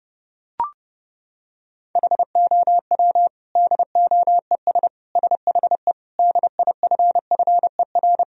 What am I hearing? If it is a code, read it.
Morse code, heard as a keyed tone, about 32 wpm, 700 Hz. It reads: HOW DOES SHE DIFFER
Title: Sentences from Top 1000 Words - Review Part 1 - 30wpm